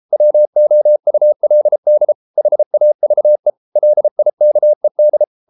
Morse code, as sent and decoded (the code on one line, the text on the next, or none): .-- --- ..- .-.. -.. / .... .- ...- . / .-.. .. -.- . -..
WOULD HAVE LIKED